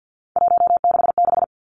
Morse code, heard as a keyed tone, about 50 wpm, 700 Hz. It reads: 166
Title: Three Challenging Letter-Number Combinations 50wpm